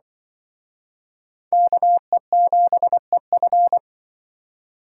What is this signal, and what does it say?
Morse code, heard as a keyed tone, about 25 wpm, 700 Hz. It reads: KE7EF
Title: Speed Racing - USA 2x2 Call Signs 20wpm